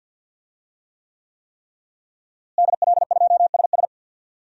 Morse code, heard as a keyed tone, about 51 wpm, 700 Hz. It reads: DL2SS
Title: DXCC-100 Call Signs - 5 Characters - Repeated 3x - 50wpm